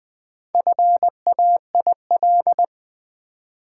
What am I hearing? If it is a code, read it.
Morse code, heard as a keyed tone, about 20 wpm, 700 Hz. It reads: FAIL
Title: Four Letter Words - 20wpm